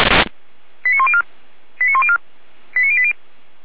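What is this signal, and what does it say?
Five-tone selective call (ZVEI), heard as quick runs of five tones, tones of about 70 ms. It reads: 89104 89104 89080
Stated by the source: Received on SDR.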